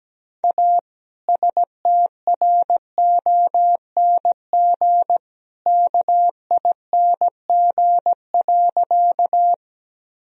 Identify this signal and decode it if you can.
Morse code, heard as a keyed tone, about 17 wpm, 700 Hz. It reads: A STRONG KING.
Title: Sentences from Top 1000 Words - Review Part 3 - 17wpm